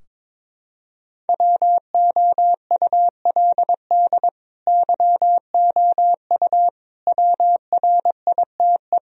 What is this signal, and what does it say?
Morse code, heard as a keyed tone, about 22 wpm, 700 Hz. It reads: WOULD YOU WRITE
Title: Sentences from Top 1000 Words - Review Part 2 - 22wpm